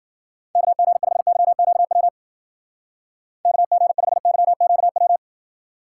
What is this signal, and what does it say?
Morse code, heard as a keyed tone, about 49 wpm, 700 Hz. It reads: KG5YQW KG5YQW
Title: Speed Racing - USA 2x3 Call Signs 40wpm